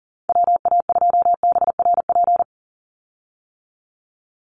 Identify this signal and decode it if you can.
Morse code, heard as a keyed tone, about 40 wpm, 690 Hz. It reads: WA2BRP
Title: USA Call Signs 40wpm